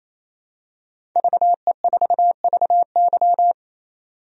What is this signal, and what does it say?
Morse code, heard as a keyed tone, about 27 wpm, 700 Hz. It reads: VE4VY
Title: DXCC-100 Call Signs - 5 Characters - 28wpm